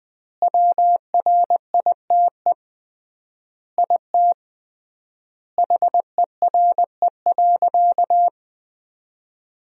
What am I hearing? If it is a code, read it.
Morse code, heard as a keyed tone, about 20 wpm, 700 Hz. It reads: WRITE IT HERE.
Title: 3x Word-Spacing - Sentences from Top 400 Words - 20wpm